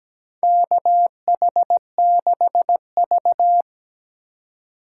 Morse code, heard as a keyed tone, about 17 wpm, 700 Hz. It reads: KH6V
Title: USA Call Signs - 4 Characters - 2x Character Spacing - 17wpm